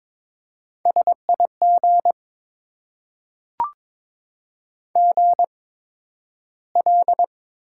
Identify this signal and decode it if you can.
Morse code, heard as a keyed tone, about 22 wpm, spaced as for 12 wpm, 700 Hz. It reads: SIG GL
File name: Top 90 QSO Elements - 8x Character Spacing - 22wpm